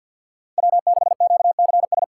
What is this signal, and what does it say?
Morse code, heard as a keyed tone, about 62 wpm, 700 Hz. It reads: W6QCS
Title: Speed Racing - USA 1x3 Call Signs 50wpm